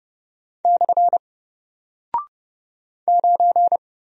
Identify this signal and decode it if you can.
Morse code, heard as a keyed tone, about 29 wpm, 700 Hz. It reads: / 9
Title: Single Character (Letter:Number:Punctuation:Prosign) 30wpm